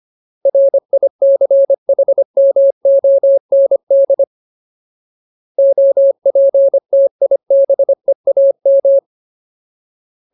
Morse code, heard as a keyed tone, about 25 wpm, 550 Hz. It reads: RICHMOND OPTIBEAM
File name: Ultimate Word List / 25WPM Ultimate